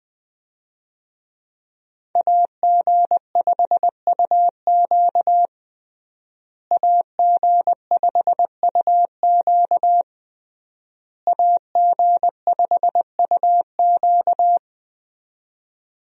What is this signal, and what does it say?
Morse code, heard as a keyed tone, about 21 wpm, 700 Hz. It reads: AG5UQ AG5UQ AG5UQ
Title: USA Call Signs - 5 Characters - Repeated 3x - 20wpm